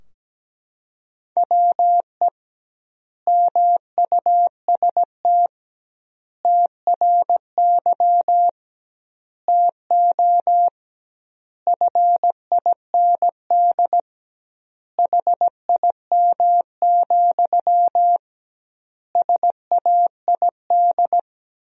Morse code, heard as a keyed tone, about 17 wpm, 700 Hz. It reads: WE MUST TRY TO FIND HIM, SAID
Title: 2x Word-Spacing - Sentences from Top 400 Words - 17wpm